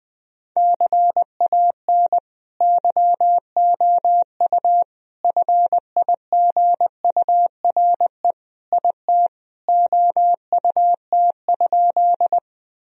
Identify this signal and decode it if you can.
Morse code, heard as a keyed tone, about 21 wpm, 700 Hz. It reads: CAN YOU FIGURE IT OUT?
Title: Sentences from Top 1000 Words - Review Part 2 - 20wpm